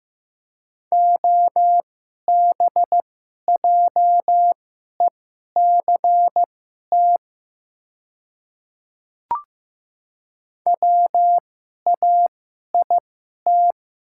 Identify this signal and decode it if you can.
Morse code, heard as a keyed tone, about 15 wpm, spaced as for 11 wpm, 700 Hz. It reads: OBJECT WAIT
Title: 2x Character Spacing - Top 500 Words - 15wpm